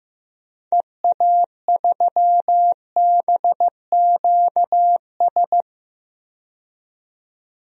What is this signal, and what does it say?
Morse code, heard as a keyed tone, about 15 wpm, 700 Hz. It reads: EA3BQS
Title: DXCC-100 Call Signs - 6 Characters - 15wpm